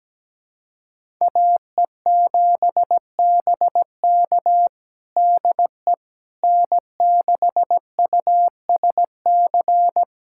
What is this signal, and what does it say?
Morse code, heard as a keyed tone, about 17 wpm, 700 Hz. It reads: AE7BK DE N6USC
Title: USA Callsign DE Callsign - 5 Characters - 17wpm